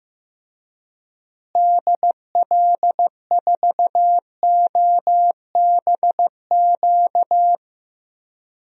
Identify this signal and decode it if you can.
Morse code, heard as a keyed tone, about 15 wpm, 700 Hz. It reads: DL4OBQ
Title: USA Call Signs - 6 Characters - Repeated 3x - 15wpm